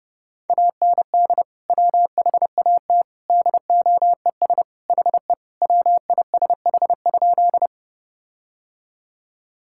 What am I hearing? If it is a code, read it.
Morse code, heard as a keyed tone, about 28 wpm, 700 Hz. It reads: AND WHAT DOES HE WISH?
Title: Sentences from Top 1600 Words 30wpm